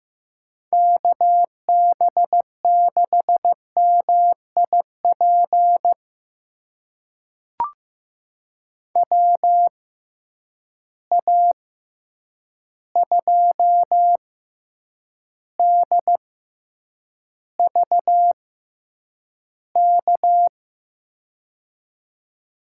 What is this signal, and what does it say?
Morse code, heard as a keyed tone, about 15 wpm, spaced as for 8 wpm, 700 Hz. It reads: KB6MIP WA2DVK
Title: USA Call Signs - 6 Characters - 6x Character Spacing - 15wpm